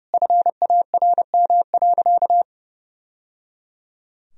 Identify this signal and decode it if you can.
Morse code, heard as a keyed tone, about 30 wpm, 700 Hz. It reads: FARM.
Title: Sentences from Top 1000 Words - Review Part 2 - 30wpm